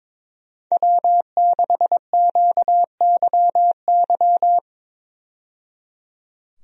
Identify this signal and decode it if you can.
Morse code, heard as a keyed tone, about 22 wpm, 700 Hz. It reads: W6QYY
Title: USA Call Signs - 5 Characters - 22wpm